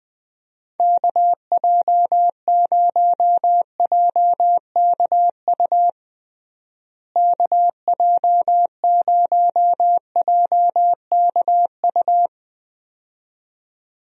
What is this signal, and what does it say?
Morse code, heard as a keyed tone, about 20 wpm, 700 Hz. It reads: KJ0JKU KJ0JKU
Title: USA Call Signs - 6 Characters - Repeated 2x - 20wpm